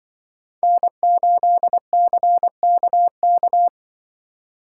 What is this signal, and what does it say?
Morse code, heard as a keyed tone, about 24 wpm, 700 Hz. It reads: N8CKK
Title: Speed Racing - USA Call Signs 20wpm